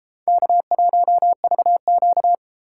Morse code, heard as a keyed tone, about 35 wpm, 700 Hz. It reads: K1VQ